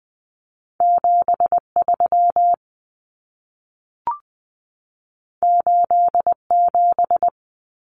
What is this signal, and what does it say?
Morse code, heard as a keyed tone, about 20 wpm, 700 Hz. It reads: 73 87
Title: Common Cut Numbers - 2-Digit 20wpm